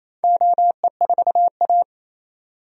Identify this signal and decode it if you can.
Morse code, heard as a keyed tone, about 28 wpm, 700 Hz. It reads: OE4A
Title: DXCC-100 Call Signs - 4 Characters - 28wpm